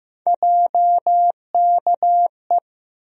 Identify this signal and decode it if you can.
Morse code, heard as a keyed tone, about 15 wpm, 700 Hz. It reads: JKE
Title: USA Call Sign Postfixes - 3 Letters 15wpm